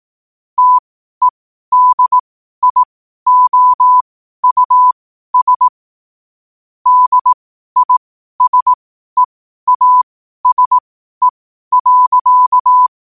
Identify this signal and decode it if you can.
Morse code, heard as a keyed tone, about 18 wpm, spaced as for 12 wpm, 1000 Hz. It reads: TEDIOUS DISEASE.
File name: qotdfiles / 12wpm